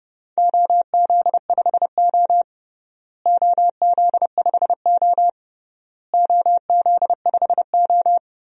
Morse code, heard as a keyed tone, about 32 wpm, 700 Hz. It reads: OZ5O OZ5O OZ5O
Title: DXCC-100 Call Signs - 4 Characters - Repeated 3x - 30wpm